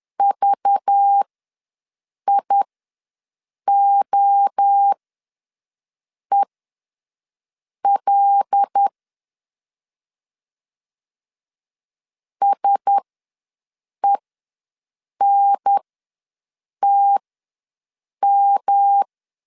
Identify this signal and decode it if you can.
Morse code, heard as a keyed tone, about 10 wpm, spaced as for 6 wpm, 800 Hz. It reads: VIOEL SENTM
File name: LEZIONI / 5 LBVC / 5 LBV ws 10wpm 3645